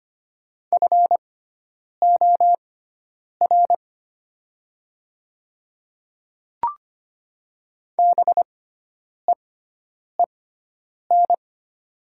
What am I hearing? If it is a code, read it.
Morse code, heard as a keyed tone, about 25 wpm, spaced as for 9 wpm, 700 Hz. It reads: FOR BEEN
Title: 6x Character Spacing - Top 100 Words - 25wpm